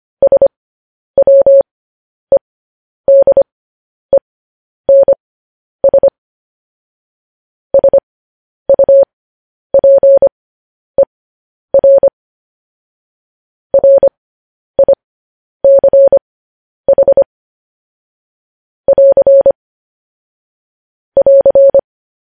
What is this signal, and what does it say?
Morse code, heard as a keyed tone, about 25 wpm, spaced as for 10 wpm, 550 Hz. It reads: SWEDENS SUPER RICH <AR> <AR>